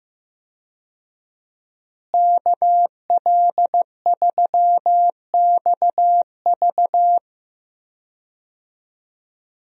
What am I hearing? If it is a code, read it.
Morse code, heard as a keyed tone, about 15 wpm, 700 Hz. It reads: KL3XV
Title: USA Call Signs - 5 Characters - Repeated 3x - 15wpm